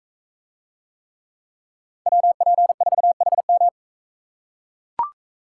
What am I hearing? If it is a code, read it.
Morse code, heard as a keyed tone, about 42 wpm, 700 Hz. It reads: WP4HM